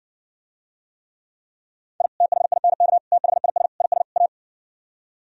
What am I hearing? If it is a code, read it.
Morse code, heard as a keyed tone, about 69 wpm, 700 Hz. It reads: I THINK THIS IS A